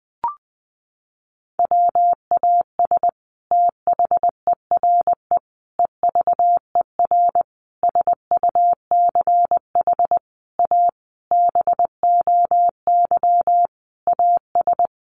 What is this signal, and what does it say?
Morse code, heard as a keyed tone, about 20 wpm, 700 Hz. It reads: WAS THERE EVER SUCH A BOY AS
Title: Sentences from Top 400 Words 20wpm